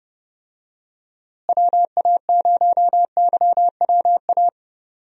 Morse code, heard as a keyed tone, about 32 wpm, 700 Hz. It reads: WA0YWA
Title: USA Call Signs - 6 Characters - Repeated 2x - 30wpm